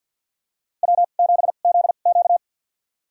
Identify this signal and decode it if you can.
Morse code, heard as a keyed tone, about 53 wpm, spaced as for 37 wpm, 700 Hz. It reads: W7ZQ
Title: USA Call Signs - 4 Characters - 2x Character Spacing - 50wpm